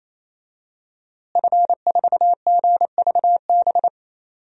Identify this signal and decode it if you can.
Morse code, heard as a keyed tone, about 27 wpm, 700 Hz. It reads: F4GVB
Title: DXCC-100 Call Signs - 5 Characters - Repeated 3x - 28wpm